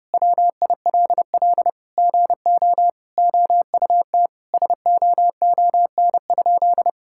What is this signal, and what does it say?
Morse code, heard as a keyed tone, about 30 wpm, 700 Hz. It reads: WILL GO OUT SOON?
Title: Sentences from Top 1600 Words 30wpm